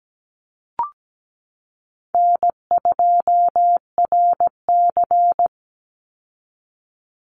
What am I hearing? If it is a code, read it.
Morse code, heard as a keyed tone, about 17 wpm, 700 Hz. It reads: N2RC
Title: Easy Call Signs Repeated 2x - 17wpm